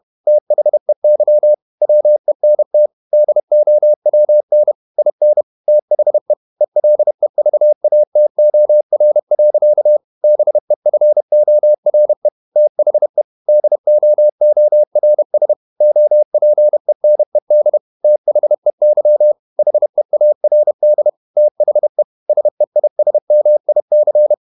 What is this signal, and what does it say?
Morse code, heard as a keyed tone, about 31 wpm, 600 Hz. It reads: THEY WENT DOWN IN THE ELEVATOR. BEFORE THE DOORS OPENED THEY HEARD THE SEISMIC
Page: Never 31wpm | CW med Gnister